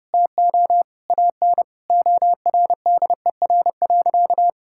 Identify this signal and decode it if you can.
Morse code, heard as a keyed tone, about 30 wpm, 700 Hz. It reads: TO AN ORDER.
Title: Sentences from Top 1600 Words 30wpm